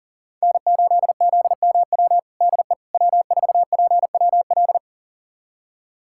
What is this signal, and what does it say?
Morse code, heard as a keyed tone, about 40 wpm, 700 Hz. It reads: N8ZMW DE W4PWL
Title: USA Callsign DE Callsign - 5 Characters - 40wpm